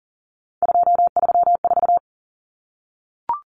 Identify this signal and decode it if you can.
Morse code, heard as a keyed tone, about 40 wpm, 700 Hz. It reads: 234
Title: Numbers - 3-Digit 40wpm